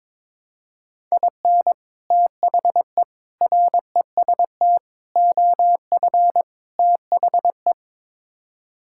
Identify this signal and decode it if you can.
Morse code, heard as a keyed tone, about 22 wpm, 700 Hz. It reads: IN THE REST OF THE
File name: practice / 22wpm / Build A Phrase - 5-Word Phrases - 22wpm